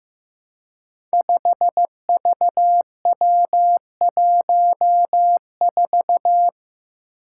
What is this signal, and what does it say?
Morse code, heard as a keyed tone, about 15 wpm, 700 Hz. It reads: 5VW14
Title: Five Challenging Letter-Number Combinations 15wpm